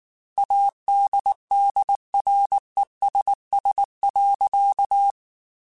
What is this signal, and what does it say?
Morse code, heard as a keyed tone, about 19 wpm, 800 Hz. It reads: ADDRESS.